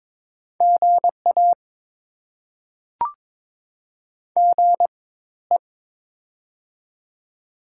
Morse code, heard as a keyed tone, about 22 wpm, spaced as for 14 wpm, 690 Hz. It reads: GA GE
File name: Top 20 QSO Elements - 4x Character Spacing - 22wpm